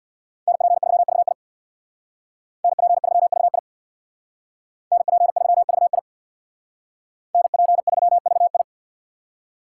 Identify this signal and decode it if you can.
Morse code, heard as a keyed tone, about 64 wpm, 700 Hz. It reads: NP3VI NP3VI NP3VI NP3VI